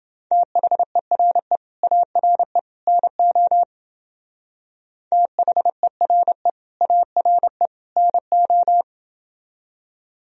Morse code, heard as a keyed tone, about 28 wpm, 700 Hz. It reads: THERE ARE NO THERE ARE NO